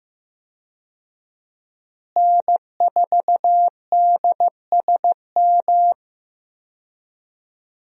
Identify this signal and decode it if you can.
Morse code, heard as a keyed tone, about 15 wpm, 700 Hz. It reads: N4DSM
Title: USA Call Signs - 5 Characters - Repeated 3x - 15wpm